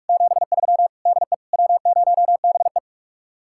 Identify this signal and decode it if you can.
Morse code, heard as a keyed tone, about 49 wpm, 700 Hz. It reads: 73 DE W0BE